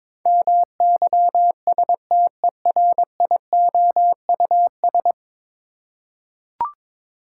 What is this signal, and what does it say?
Morse code, heard as a keyed tone, about 22 wpm, 700 Hz. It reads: MYSTERIOUS
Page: morse-code-ninja-practice-22wpm